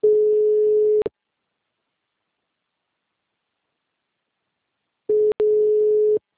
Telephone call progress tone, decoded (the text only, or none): ringback tone